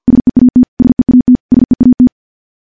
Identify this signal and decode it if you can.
Morse code, heard as a keyed tone, about 50 wpm, 250 Hz. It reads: SEAT SEAT SEAT
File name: morse / morse-training-50WPM